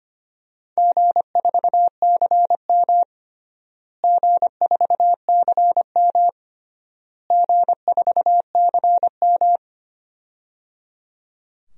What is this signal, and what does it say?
Morse code, heard as a keyed tone, about 25 wpm, 700 Hz. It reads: G4CM G4CM G4CM
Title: DXCC-100 Call Signs - 4 Characters - Repeated 3x - 25wpm